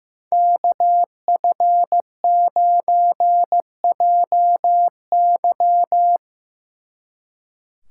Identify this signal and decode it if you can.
Morse code, heard as a keyed tone, about 17 wpm, 700 Hz. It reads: KF9JY
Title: Speed Racing - USA 2x2 Call Signs 15wpm